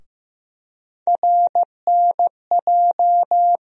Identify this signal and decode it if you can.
Morse code, heard as a keyed tone, about 15 wpm, 700 Hz. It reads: RNJ
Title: USA Call Sign Postfixes - 3 Letters 15wpm